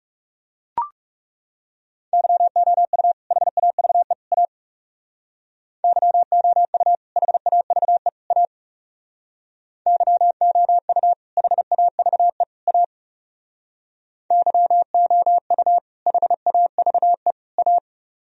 Morse code, heard as a keyed tone, about 36 wpm, 700 Hz. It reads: YOU HAVE A YOU HAVE A YOU HAVE A YOU HAVE A
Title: Speed-Racing - Sets of 3 Words 30wpm